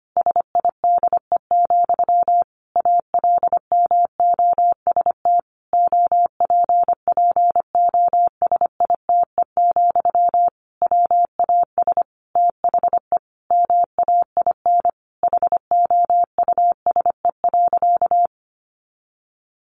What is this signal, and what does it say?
Morse code, heard as a keyed tone, about 25 wpm, 700 Hz. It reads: SIDE, ALMOST OPPOSITE, WAS THE MAIN HOUSE.
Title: Sentences from Top 1300 Words 25wpm